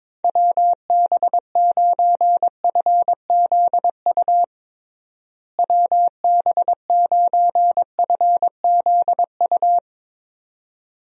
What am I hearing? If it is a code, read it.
Morse code, heard as a keyed tone, about 22 wpm, 700 Hz. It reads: WB9FZU WB9FZU
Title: USA Call Signs - 6 Characters - Repeated 2x - 22wpm